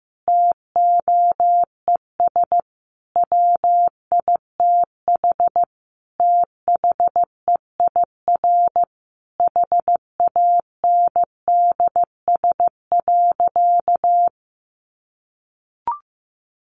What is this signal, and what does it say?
Morse code, heard as a keyed tone, about 15 wpm, 700 Hz. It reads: TOES WITH THEIR HANDS.
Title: Easy Phrases 15wpm